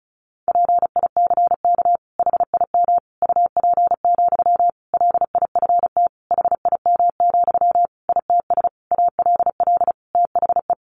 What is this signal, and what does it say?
Morse code, heard as a keyed tone, about 34 wpm, 700 Hz. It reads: PICK HIM UP, LIFT HIM, ITS ALL THE
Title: Sentences from Top 1000 Words 35wpm